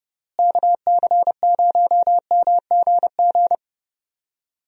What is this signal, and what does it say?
Morse code, heard as a keyed tone, about 30 wpm, 700 Hz. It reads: KC0MGG
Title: USA Call Signs - 6 Characters - Repeated 2x - 30wpm